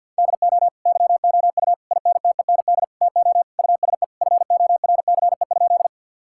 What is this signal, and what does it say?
Morse code, heard as a keyed tone, about 50 wpm, 700 Hz. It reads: DO YOU INTEND TO USE FORCE?
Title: Sentences from Top 1600 Words 50wpm